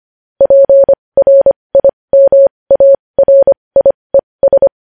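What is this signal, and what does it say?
Morse code, heard as a keyed tone, about 25 wpm, spaced as for 20 wpm, 550 Hz. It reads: PRIMARIES
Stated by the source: Morse code news headlines sent at 20 wpm.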